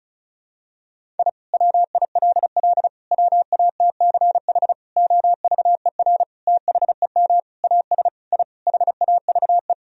Morse code, heard as a keyed tone, about 34 wpm, 700 Hz. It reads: I WILL WATCH OVER THEM AS I HAVE
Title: Sentences from Top 1000 Words - Review Part 1 - 35wpm